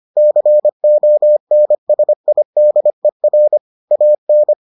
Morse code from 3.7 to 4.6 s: AN

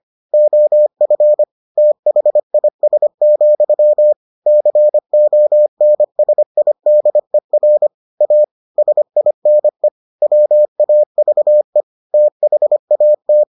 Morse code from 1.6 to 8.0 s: THIS, CONSIDER

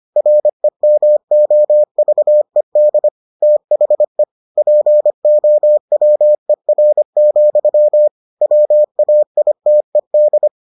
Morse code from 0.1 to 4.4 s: REMOVED THE

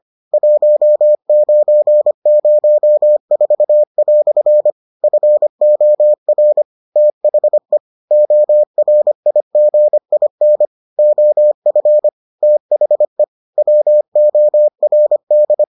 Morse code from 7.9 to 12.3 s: ORIGIN OF

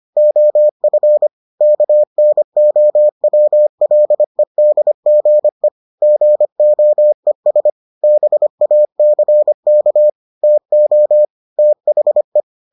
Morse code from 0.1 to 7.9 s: OF KNOWLEDGE GOES

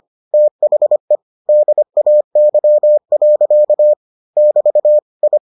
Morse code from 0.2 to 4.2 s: THE DAY.

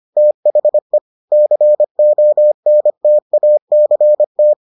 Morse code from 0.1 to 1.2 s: THE

